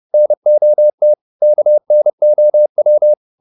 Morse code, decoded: NOT KNOW

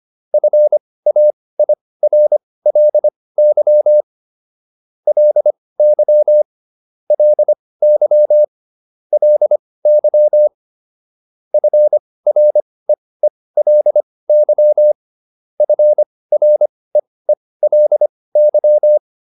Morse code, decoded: FAIRLY LY LY LY FREELY FREELY

